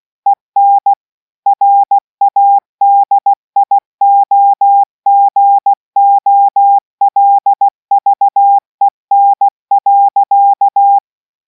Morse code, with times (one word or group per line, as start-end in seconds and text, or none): 0.3-0.9 EN
1.5-11.0 RADIOGOLVEN.